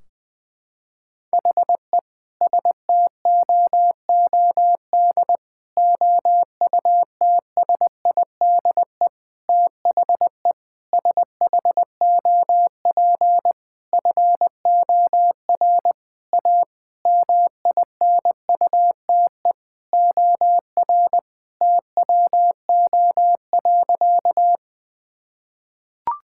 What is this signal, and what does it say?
Morse code, heard as a keyed tone, about 20 wpm, 700 Hz. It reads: HE STOOD OUTSIDE THE SHOP FOR A MINUTE OR TWO.
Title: Sentences from Top 1600 Words 20wpm